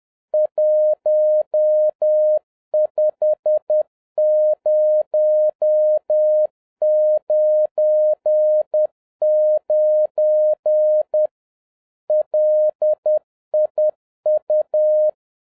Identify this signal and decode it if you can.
Morse code, heard as a keyed tone, about 10 wpm, 610 Hz. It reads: 15099 LIU